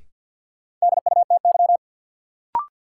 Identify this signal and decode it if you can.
Morse code, heard as a keyed tone, about 53 wpm, 700 Hz. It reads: DUTY
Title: Sentences from Top 1500 Words - New Word Drill - 50wpm